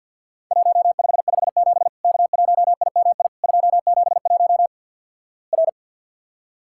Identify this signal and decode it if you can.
Morse code, heard as a keyed tone, about 53 wpm, 700 Hz. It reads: 1557 K1IMI 271 R
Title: Worked All Europe DX Contest - QTC Exchange - 50wpm